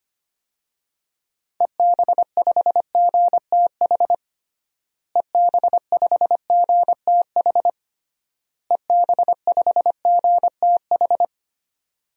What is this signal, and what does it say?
Morse code, heard as a keyed tone, about 25 wpm, 700 Hz. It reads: EB5GTH EB5GTH EB5GTH
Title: DXCC-100 Call Signs - 6 Characters - Repeated 3x - 25wpm